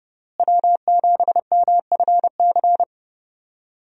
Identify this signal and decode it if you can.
Morse code, heard as a keyed tone, about 30 wpm, 700 Hz. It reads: W7MFC
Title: USA Call Signs - 5 Characters - Repeated 3x - 30wpm